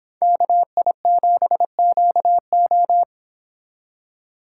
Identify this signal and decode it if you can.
Morse code, heard as a keyed tone, about 26 wpm, 700 Hz. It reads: KI7QO